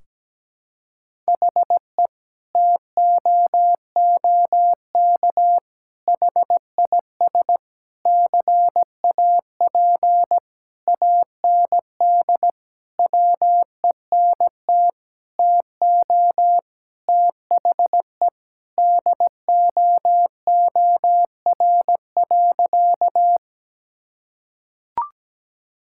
Morse code, heard as a keyed tone, about 17 wpm, 700 Hz. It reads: HE TOOK HIS CAP AND WENT TO THE DOOR.
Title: Sentences from Top 1600 Words 17wpm